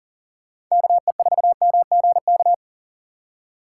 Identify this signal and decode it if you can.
Morse code, heard as a keyed tone, about 40 wpm, 700 Hz. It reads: KE4MGK
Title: USA Call Signs - 4x Character Spacing - 40wpm